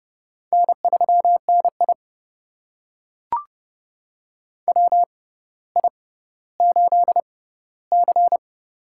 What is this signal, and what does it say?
Morse code, heard as a keyed tone, about 30 wpm, spaced as for 15 wpm, 700 Hz. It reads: N3NI WI8C
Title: USA Call Signs - 4 Characters - 6x Character Spacing - 30wpm